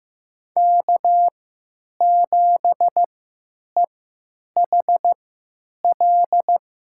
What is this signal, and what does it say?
Morse code, heard as a keyed tone, about 15 wpm, spaced as for 8 wpm, 700 Hz. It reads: K7EHL
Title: USA Call Signs - 5 Characters - 3x Character Spacing - 15wpm